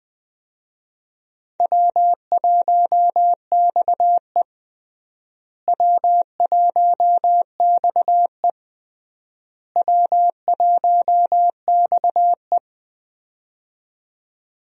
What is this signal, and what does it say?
Morse code, heard as a keyed tone, about 20 wpm, 700 Hz. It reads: W1XE W1XE W1XE
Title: USA Call Signs - 4 Characters - Repeated 3x - 20wpm